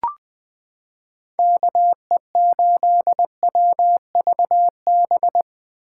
Morse code, heard as a keyed tone, about 21 wpm, 700 Hz. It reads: KE8WVB
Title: 2023 POTA Activator Call Signs - 20wpm